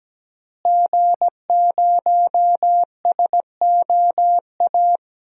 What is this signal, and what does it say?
Morse code, heard as a keyed tone, about 17 wpm, 700 Hz. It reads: G0SOA
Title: DXCC-100 Call Signs - 5 Characters - Repeated 3x - 17wpm